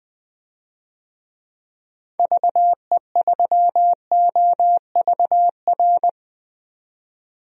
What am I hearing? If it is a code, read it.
Morse code, heard as a keyed tone, about 20 wpm, 700 Hz. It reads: VE3OVR
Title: DXCC-100 Call Signs - 6 Characters - 20wpm